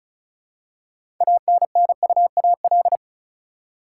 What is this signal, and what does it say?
Morse code, heard as a keyed tone, about 36 wpm, 700 Hz. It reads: ANNUAL
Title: Six Letter Words - 35wpm